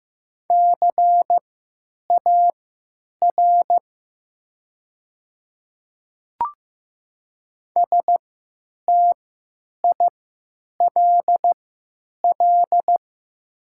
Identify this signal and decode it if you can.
Morse code, heard as a keyed tone, about 15 wpm, spaced as for 9 wpm, 690 Hz. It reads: CAR STILL
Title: 3x Character Spacing - Top 300 Words - 15wpm